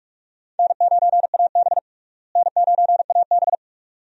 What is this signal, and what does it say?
Morse code, heard as a keyed tone, about 45 wpm, 700 Hz. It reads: N9AB N9AB
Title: USA Call Signs - 4 Characters - Repeated 2x - 45wpm